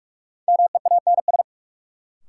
Morse code, heard as a keyed tone, about 46 wpm, 700 Hz. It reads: MEANS
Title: Sets of 5 Words - New Word Drill - 45wpm